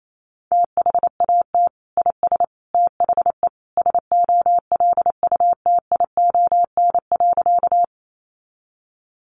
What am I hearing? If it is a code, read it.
Morse code, heard as a keyed tone, about 28 wpm, 700 Hz. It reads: THAT IS THE SOLUTION.
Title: Sentences from Top 1300 Words 28wpm